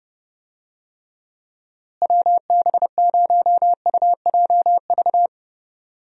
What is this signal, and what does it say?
Morse code, heard as a keyed tone, about 30 wpm, 700 Hz. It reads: WB0UJV
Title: USA Call Signs - 6 Characters - Repeated 2x - 30wpm